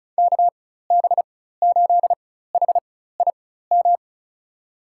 Morse code, heard as a keyed tone, about 34 wpm, spaced as for 16 wpm, 700 Hz. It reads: KB8HIM
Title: USA Call Signs - 6 Characters - 4x Character Spacing - 35wpm